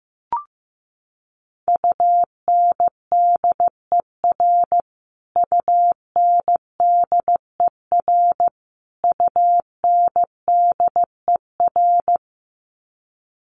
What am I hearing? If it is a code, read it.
Morse code, heard as a keyed tone, about 15 wpm, 700 Hz. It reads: UNDER UNDER UNDER
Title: Top 300 Words Repeated 15wpm